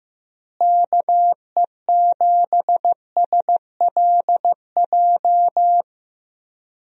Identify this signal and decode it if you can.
Morse code, heard as a keyed tone, about 16 wpm, 700 Hz. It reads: KE7SLJ
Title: USA Call Signs - 6 Characters - Repeated 3x - 15wpm